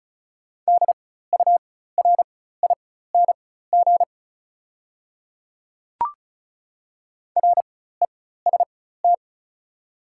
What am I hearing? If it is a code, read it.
Morse code, heard as a keyed tone, about 34 wpm, spaced as for 16 wpm, 690 Hz. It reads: DURING REST
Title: 4x Character Spacing - Top 500 Words - 35wpm